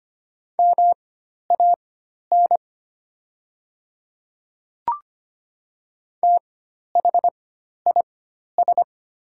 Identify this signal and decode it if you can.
Morse code, heard as a keyed tone, about 25 wpm, spaced as for 12 wpm, 700 Hz. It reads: MAN THIS
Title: 4x Character Spacing - Top 300 Words - 25wpm